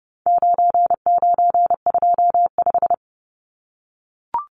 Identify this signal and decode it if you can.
Morse code, heard as a keyed tone, about 29 wpm, 700 Hz. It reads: 9925
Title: Numbers - 4-Digit 30wpm